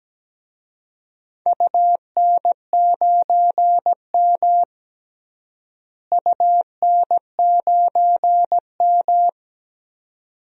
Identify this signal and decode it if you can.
Morse code, heard as a keyed tone, about 17 wpm, 700 Hz. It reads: UN9M UN9M
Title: DXCC-100 Call Signs - 4 Characters - Repeated 2x - 17wpm